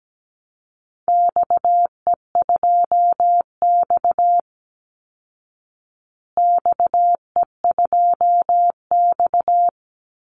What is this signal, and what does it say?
Morse code, heard as a keyed tone, about 17 wpm, 700 Hz. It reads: XE2X XE2X
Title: Call Signs Repeated 3x - 17wpm